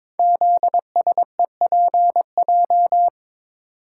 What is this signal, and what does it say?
Morse code, heard as a keyed tone, about 22 wpm, 700 Hz. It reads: ZSEPJ
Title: Code Groups - 5 Letters - 22wpm